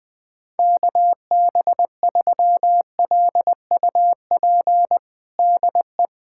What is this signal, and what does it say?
Morse code, heard as a keyed tone, about 20 wpm, 700 Hz. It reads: KB3LUP DE
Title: USA Callsign DE Callsign 20wpm